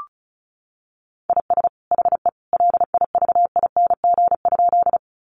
Morse code, heard as a keyed tone, about 34 wpm, 700 Hz. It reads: IS HE LIVING?
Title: Sentences from Top 1100 Words 35wpm